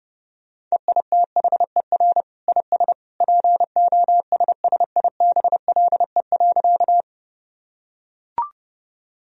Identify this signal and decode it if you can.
Morse code, heard as a keyed tone, about 30 wpm, 700 Hz. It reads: EITHER IS POSSIBLE.
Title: Sentences from Top 1000 Words - Review Part 4 - 30wpm